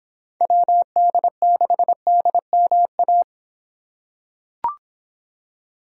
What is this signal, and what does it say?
Morse code, heard as a keyed tone, about 24 wpm, 700 Hz. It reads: WD6DMA